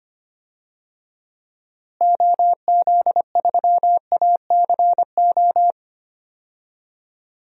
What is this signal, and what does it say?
Morse code, heard as a keyed tone, about 25 wpm, 700 Hz. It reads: OZ3ACO
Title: DXCC-100 Call Signs - 6 Characters - Repeated 3x - 25wpm